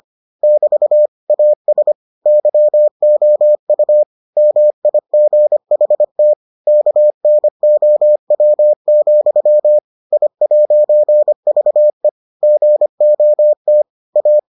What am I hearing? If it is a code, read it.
Morse code, heard as a keyed tone, about 25 wpm, 600 Hz. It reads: <BT> AS YOU MIGHT KNOW, I'VE GOT A